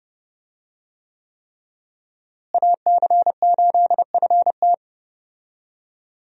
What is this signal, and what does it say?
Morse code, heard as a keyed tone, about 32 wpm, 700 Hz. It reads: AC8FT
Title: USA Call Signs - 5 Characters - Repeated 3x - 30wpm